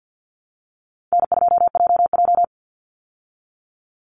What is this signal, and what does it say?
Morse code, heard as a keyed tone, about 50 wpm, 700 Hz. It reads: N2JJ
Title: Easy Call Signs Repeated 2x - 50wpm